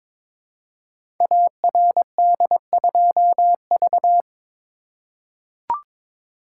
Morse code, heard as a keyed tone, about 22 wpm, 700 Hz. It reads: ARD2V
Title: Code Groups - 5 Letter:Numbers - 22wpm